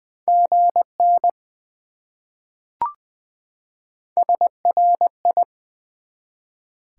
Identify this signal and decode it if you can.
Morse code, heard as a keyed tone, about 20 wpm, 700 Hz. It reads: GN SRI
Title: Top 90 QSO Elements 20wpm